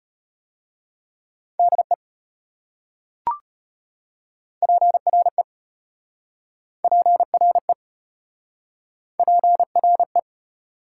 Morse code, heard as a keyed tone, about 34 wpm, 690 Hz. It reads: DE PRE PRE PRE